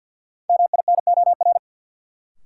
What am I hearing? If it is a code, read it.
Morse code, heard as a keyed tone, about 51 wpm, 700 Hz. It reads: MINOR
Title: Sentences from Top 1500 Words - New Word Drill Alt - 50wpm